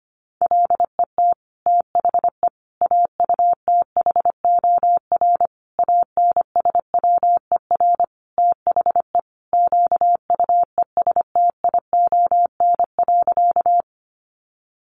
Morse code, heard as a keyed tone, about 25 wpm, 700 Hz. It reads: LET THE AUTHOR ANSWER THE QUESTION.
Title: Sentences from Top 1200 Words 25wpm